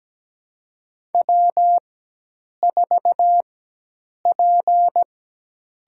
Morse code, heard as a keyed tone, about 17 wpm, spaced as for 8 wpm, 700 Hz. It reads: W4P
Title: USA Call Signs - 4x Character Spacing - 17wpm